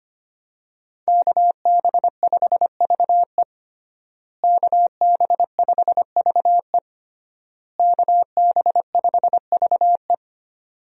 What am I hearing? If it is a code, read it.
Morse code, heard as a keyed tone, about 25 wpm, 700 Hz. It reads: KB5VE KB5VE KB5VE
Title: USA Call Signs - 5 Characters - Repeated 3x - 25wpm